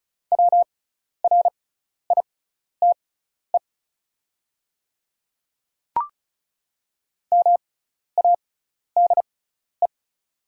Morse code, heard as a keyed tone, about 36 wpm, spaced as for 12 wpm, 690 Hz. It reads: WRITE MADE
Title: 6x Character Spacing - Top 300 Words - 35wpm